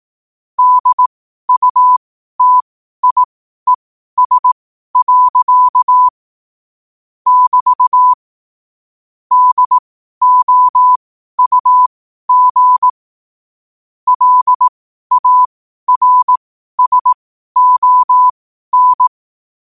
Morse code, heard as a keyed tone, about 18 wpm, spaced as for 12 wpm, 1000 Hz. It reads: DUTIES. <BT> DOUG LARSON
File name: qotdfiles / 12wpm